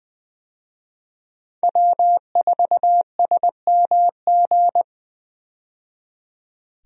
Morse code, heard as a keyed tone, about 20 wpm, 700 Hz. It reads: W4SMG
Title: USA Call Signs - 5 Characters - 20wpm